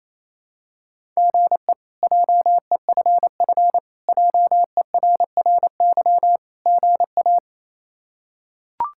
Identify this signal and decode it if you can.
Morse code, heard as a keyed tone, about 28 wpm, 700 Hz. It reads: GE JEFF JERRY GA
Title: K1USN SST QSO Practice - 28wpm